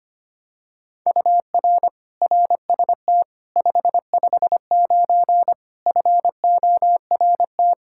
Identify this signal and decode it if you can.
Morse code, heard as a keyed tone, about 25 wpm, 700 Hz. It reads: UR RST 559 FORT
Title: Practice QSO - Exchange 1 - Beginning - 25wpm